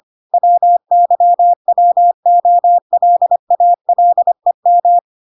Morse code, heard as a keyed tone, about 25 wpm, 700 Hz. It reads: WYWOLALEM